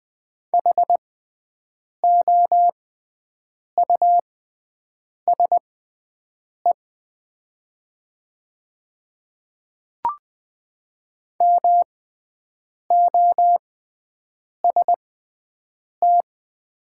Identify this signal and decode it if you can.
Morse code, heard as a keyed tone, about 20 wpm, spaced as for 7 wpm, 700 Hz. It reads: HOUSE MOST
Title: 6x Character Spacing - Top 200 Words - 20wpm